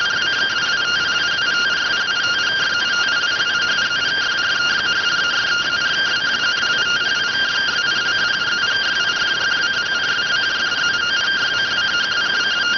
Two-tone teleprinter signal, about 170 Hz shift.